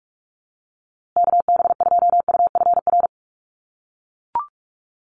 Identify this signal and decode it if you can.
Morse code, heard as a keyed tone, about 44 wpm, 700 Hz. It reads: KB2UFR
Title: USA Call Signs Repeated 3x 45wpm